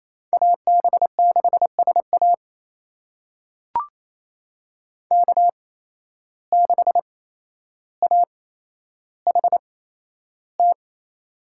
Morse code, heard as a keyed tone, about 28 wpm, spaced as for 12 wpm, 700 Hz. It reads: AB6SA K6AHT
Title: USA Call Signs - 5 Characters - 8x Character Spacing - 28wpm